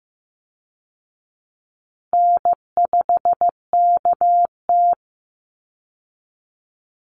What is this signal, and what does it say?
Morse code, heard as a keyed tone, about 14 wpm, 700 Hz. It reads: N5KT
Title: Call Signs Repeated 3x - 15wpm